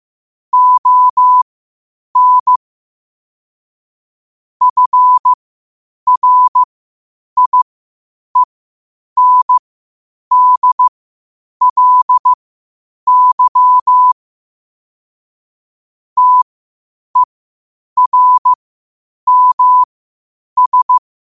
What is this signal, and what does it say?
Morse code, heard as a keyed tone, about 15 wpm, spaced as for 8 wpm, 1000 Hz. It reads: ON FRIENDLY TERMS
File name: qotdfiles / 8wpm